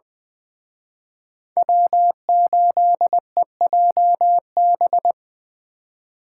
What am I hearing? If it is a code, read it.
Morse code, heard as a keyed tone, about 20 wpm, 700 Hz. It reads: W8EJB
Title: Speed Racing - USA 1x3 Call Signs 17wpm